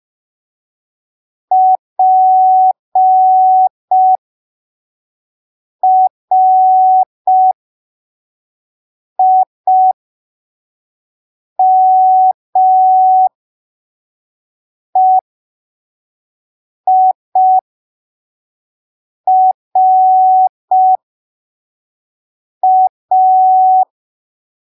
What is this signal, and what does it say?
Morse code, heard as a keyed tone, about 5 wpm, spaced as for 3 wpm, 750 Hz. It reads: PRIMEIRA
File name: arquivos / texto / texto 29 05ppm